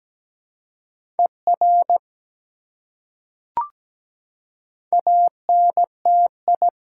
Morse code, heard as a keyed tone, about 17 wpm, 700 Hz. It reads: ER ANTI
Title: Prefixes and Suffixes 17wpm